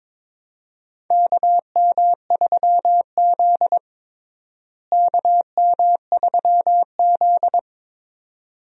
Morse code, heard as a keyed tone, about 22 wpm, 700 Hz. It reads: KM3Z KM3Z
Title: USA Call Signs - 4 Characters - Repeated 2x - 22wpm